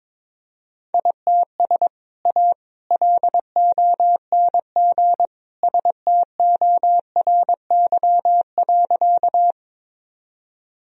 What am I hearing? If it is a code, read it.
Morse code, heard as a keyed tone, about 22 wpm, 700 Hz. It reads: ITS A LONG STORY.
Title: Sentences from Top 1000 Words - Review Part 1 - 22wpm